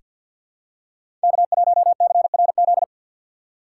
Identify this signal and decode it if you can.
Morse code, heard as a keyed tone, about 49 wpm, 700 Hz. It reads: K1CRZ
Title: USA Call Signs - 6x Character Spacing - 50wpm